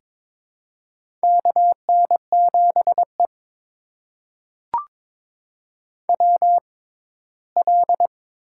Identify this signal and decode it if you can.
Morse code, heard as a keyed tone, about 22 wpm, spaced as for 15 wpm, 700 Hz. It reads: KN7E WL
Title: USA Call Signs - 4 Characters - 6x Character Spacing - 22wpm